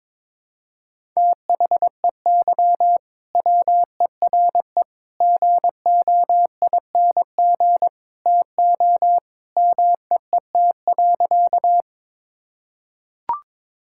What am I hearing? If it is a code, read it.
Morse code, heard as a keyed tone, about 21 wpm, 700 Hz. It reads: THEY WERE GOING TO MEET.
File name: practice / 22wpm / Sentences with 3-Word Phrases - 22wpm